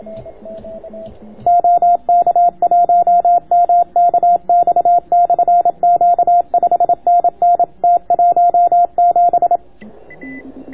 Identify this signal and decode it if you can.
Morse code, heard as a keyed tone, about 27 wpm, 690 Hz. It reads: OK1MKX/Q 5NN T17